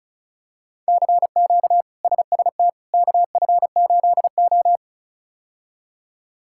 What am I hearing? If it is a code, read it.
Morse code, heard as a keyed tone, about 36 wpm, 700 Hz. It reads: CQ SST KF8O
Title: K1USN SST QSO Practice - 35wpm